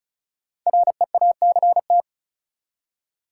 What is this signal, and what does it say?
Morse code, heard as a keyed tone, about 36 wpm, 700 Hz. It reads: REACT
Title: Five Letter Words - 35wpm